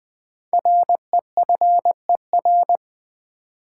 Morse code, heard as a keyed tone, about 20 wpm, 700 Hz. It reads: REFER
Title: Sentences from Top 1000 Words - New Word Drill - 20wpm